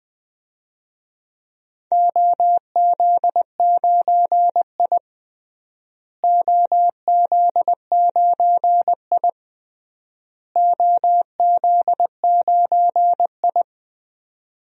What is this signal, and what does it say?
Morse code, heard as a keyed tone, about 20 wpm, 700 Hz. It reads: OZ9I OZ9I OZ9I
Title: DXCC-100 Call Signs - 4 Characters - Repeated 3x - 20wpm